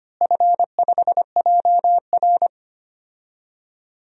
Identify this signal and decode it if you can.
Morse code, heard as a keyed tone, about 25 wpm, 700 Hz. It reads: F5JR
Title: DXCC-100 Call Signs - 4 Characters - Repeated 3x - 25wpm